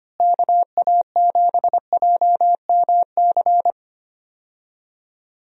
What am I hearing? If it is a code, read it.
Morse code, heard as a keyed tone, about 25 wpm, 700 Hz. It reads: KA7JMC
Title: USA Call Signs - 6 Characters - Repeated 2x - 25wpm